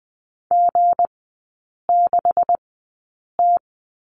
Morse code, heard as a keyed tone, about 20 wpm, spaced as for 8 wpm, 700 Hz. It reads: G6T
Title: 159 Lesson 26e - Letter g - Three Characters 20wpm